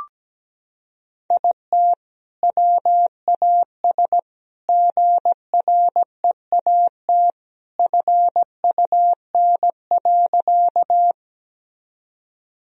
Morse code, heard as a keyed tone, about 17 wpm, 700 Hz. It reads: IT WAS GREAT FUN.
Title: Sentences from Top 1000 Words - Review Part 4 - 17wpm